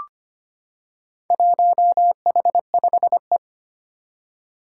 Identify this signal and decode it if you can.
Morse code, heard as a keyed tone, about 25 wpm, 700 Hz. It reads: 1H5E
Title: Four Challenging Letter-Number Combinations 25wpm